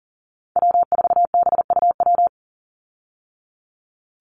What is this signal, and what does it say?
Morse code, heard as a keyed tone, about 40 wpm, 700 Hz. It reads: W4BUW
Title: Call Signs Repeated 2x - 40wpm